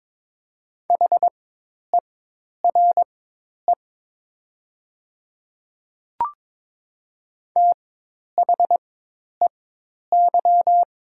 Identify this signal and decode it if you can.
Morse code, heard as a keyed tone, about 22 wpm, spaced as for 10 wpm, 690 Hz. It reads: HERE THEY
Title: 4x Character Spacing - Top 200 Words - 22wpm